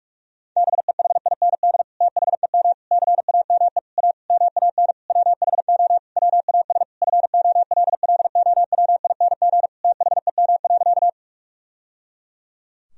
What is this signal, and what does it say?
Morse code, heard as a keyed tone, about 46 wpm, 700 Hz. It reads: BEHIND THEM CAME A MAN WHO WAS FOLLOWING THEM.
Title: Sentences from Top 1000 Words - Review Part 4 - 45wpm